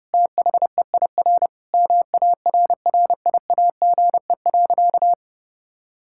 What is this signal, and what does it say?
Morse code, heard as a keyed tone, about 32 wpm, 700 Hz. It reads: THEIR MARRIAGE.
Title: Sentences from Top 1000 Words - Review Part 6 - 30wpm